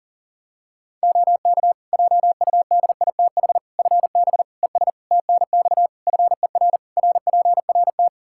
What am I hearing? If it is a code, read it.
Morse code, heard as a keyed tone, about 40 wpm, 700 Hz. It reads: OK JUDITH FB ES TNX FER RPRT
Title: Practice QSO - Exchange 2 - Info - 40wpm